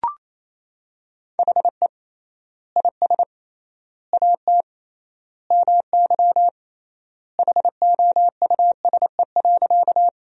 Morse code, heard as a keyed tone, about 28 wpm, 700 Hz. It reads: HE IS AT MY HOUSE.
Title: 3x Word-Spacing - Sentences from Top 200 Words - 28wpm